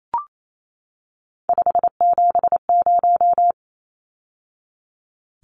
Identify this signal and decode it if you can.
Morse code, heard as a keyed tone, about 28 wpm, 700 Hz. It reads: 570
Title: Numbers - 3-Digit 28wpm